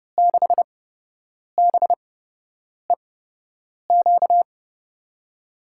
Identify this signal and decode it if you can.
Morse code, heard as a keyed tone, about 30 wpm, spaced as for 8 wpm, 700 Hz. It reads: 6BEQ
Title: USA Call Signs - 6 Characters - 8x Character Spacing - 30wpm